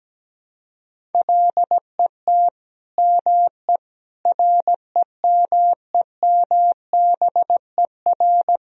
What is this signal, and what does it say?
Morse code, heard as a keyed tone, about 17 wpm, 700 Hz. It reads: LET ME REMEMBER
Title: Sentences from Top 400 Words and Abbreviations 17wpm